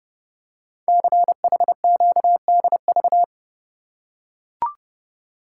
Morse code, heard as a keyed tone, about 30 wpm, 700 Hz. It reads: CHQDV
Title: Code Groups - 5 Letter:Numbers - 30wpm